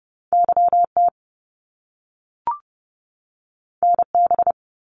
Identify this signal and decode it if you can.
Morse code, heard as a keyed tone, about 30 wpm, 700 Hz. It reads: YT NB
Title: Canadian Provinces and Territories 30wpm